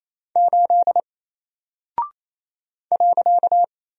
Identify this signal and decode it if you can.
Morse code, heard as a keyed tone, about 27 wpm, 700 Hz. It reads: 8 .
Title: Single Character (Letter:Number:Punctuation:Prosign) 28wpm